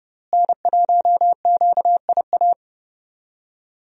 Morse code, heard as a keyed tone, about 31 wpm, 700 Hz. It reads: N1QIA
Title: USA Call Signs - 5 Characters - Repeated 3x - 30wpm